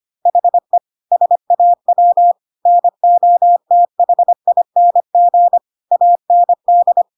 Morse code, heard as a keyed tone, about 25 wpm, 700 Hz. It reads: HE SAW NOTHING AND
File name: Pinocchio / Pinocchio ch3 25 25 ik1hge